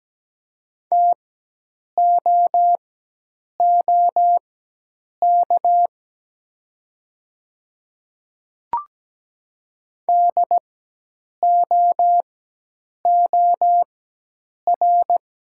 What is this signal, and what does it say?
Morse code, heard as a keyed tone, about 17 wpm, spaced as for 8 wpm, 700 Hz. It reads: TOOK DOOR
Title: 4x Character Spacing - Top 500 Words - 17wpm